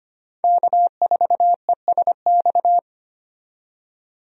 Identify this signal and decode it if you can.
Morse code, heard as a keyed tone, about 25 wpm, 700 Hz. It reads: K4ESX
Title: USA Call Signs - 5 Characters - 25wpm